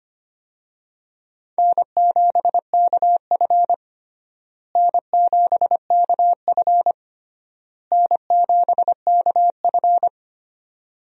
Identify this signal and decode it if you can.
Morse code, heard as a keyed tone, about 25 wpm, 700 Hz. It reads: N7KF N7KF N7KF
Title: USA Call Signs - 4 Characters - Repeated 3x - 25wpm